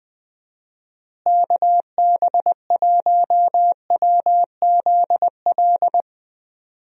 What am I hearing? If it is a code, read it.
Morse code, heard as a keyed tone, about 21 wpm, 700 Hz. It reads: KB1WZL
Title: USA Call Signs - 6 Characters - Repeated 2x - 20wpm